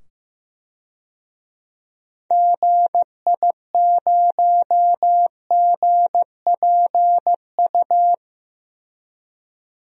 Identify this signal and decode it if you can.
Morse code, heard as a keyed tone, about 15 wpm, 700 Hz. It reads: GI0GPU
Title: DXCC-100 Call Signs - 6 Characters - 15wpm